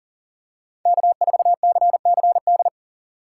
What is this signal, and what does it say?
Morse code, heard as a keyed tone, about 40 wpm, 700 Hz. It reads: K4CCD
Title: USA Call Signs - 5 Characters - 4x Character Spacing - 40wpm